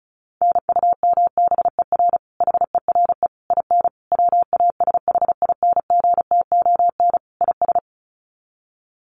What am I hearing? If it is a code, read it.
Morse code, heard as a keyed tone, about 34 wpm, 700 Hz. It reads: NUMBER HERE IN WASHINGTON IS
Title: Sets of 5 Words Encore2 35wpm